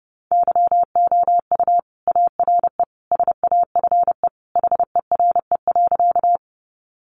Morse code, heard as a keyed tone, about 30 wpm, 700 Hz. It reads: YOU ARE SAFE HERE.
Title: Sentences from Top 900 Words 30wpm